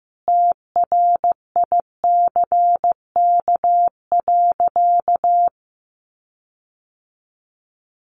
Morse code, heard as a keyed tone, about 15 wpm, 700 Hz. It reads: TRICK.
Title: Sentences from Top 1500 Words 15wpm